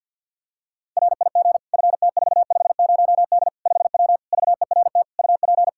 Morse code, heard as a keyed tone, about 47 wpm, 700 Hz. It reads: RIG FT450D 5W VERT UP
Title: Practice QSO - Exchange 2 - Info - 50wpm